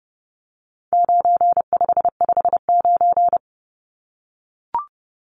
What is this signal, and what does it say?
Morse code, heard as a keyed tone, about 30 wpm, 700 Hz. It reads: 9559
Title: Numbers - 4-Digit 30wpm